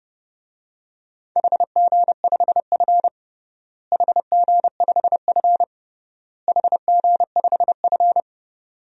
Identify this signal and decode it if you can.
Morse code, heard as a keyed tone, about 30 wpm, 700 Hz. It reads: HG5F HG5F HG5F
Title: DXCC-100 Call Signs - 4 Characters - Repeated 3x - 30wpm